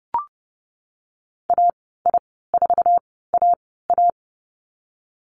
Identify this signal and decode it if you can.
Morse code, heard as a keyed tone, about 30 wpm, spaced as for 17 wpm, 700 Hz. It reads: AI4AA
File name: Lesson 19g - Letter w - Call Signs 30wpm